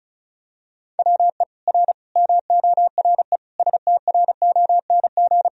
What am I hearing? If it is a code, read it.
Morse code, heard as a keyed tone, about 36 wpm, 700 Hz. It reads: WE R MORE STRONG
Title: Sentences from Top 500 Words and Abbreviations 35wpm